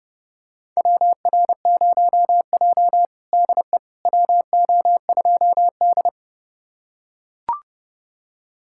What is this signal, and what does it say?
Morse code, heard as a keyed tone, about 29 wpm, 700 Hz. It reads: WR0J DE WO2D
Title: USA Callsign DE Callsign - 4 Characters - 30wpm